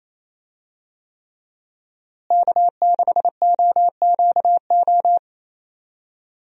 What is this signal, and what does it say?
Morse code, heard as a keyed tone, about 28 wpm, 700 Hz. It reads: K6OQO
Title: USA Call Signs - 5 Characters - Repeated 3x - 28wpm